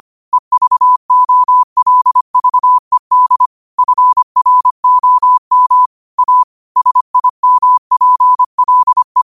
Morse code, heard as a keyed tone, about 25 wpm, 1000 Hz. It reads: EVOLVED FROM A SIMPLE
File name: qotdfiles / 25wpm